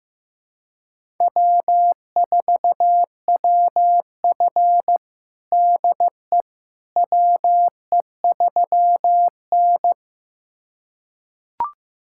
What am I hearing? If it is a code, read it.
Morse code, heard as a keyed tone, about 15 wpm, 700 Hz. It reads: W4WF DE WE3N
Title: USA Callsign DE Callsign - 4 Characters - 15wpm